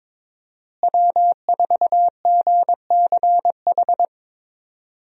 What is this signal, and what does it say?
Morse code, heard as a keyed tone, about 23 wpm, 700 Hz. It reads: W4GCH
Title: USA Call Signs - 5 Characters - Repeated 3x - 22wpm